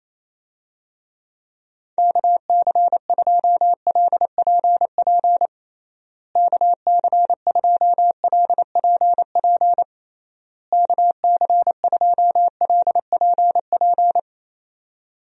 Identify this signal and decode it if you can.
Morse code, heard as a keyed tone, about 28 wpm, 700 Hz. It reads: KC2LPP KC2LPP KC2LPP
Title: USA Call Signs - 6 Characters - Repeated 3x - 28wpm